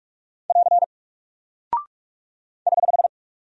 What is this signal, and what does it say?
Morse code, heard as a keyed tone, about 46 wpm, 700 Hz. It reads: <AR> <HH>
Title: Single Character (Letter:Number:Punctuation:Prosign) 45wpm